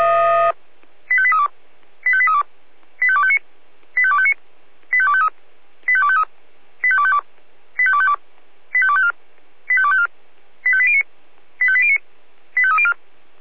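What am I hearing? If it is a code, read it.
Five-tone selective call (ZVEI), heard as quick runs of five tones, tones of about 70 ms. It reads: 85832 85832 85259 85259 85253 85253 85252 85252 85254 85254 85809 85809 85304
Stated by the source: ZVEI five-tone selective call 85840 received by SDR.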